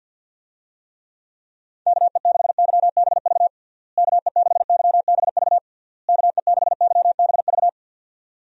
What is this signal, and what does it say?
Morse code, heard as a keyed tone, about 50 wpm, 700 Hz. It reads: KE6YBV KE6YBV KE6YBV
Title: USA Call Signs - 6 Characters - Repeated 3x - 50wpm